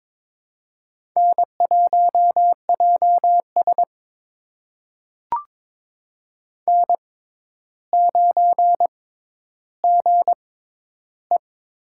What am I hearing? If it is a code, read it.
Morse code, heard as a keyed tone, about 22 wpm, spaced as for 11 wpm, 700 Hz. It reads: N1JS N9GE
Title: USA Call Signs - 4 Characters - 6x Character Spacing - 22wpm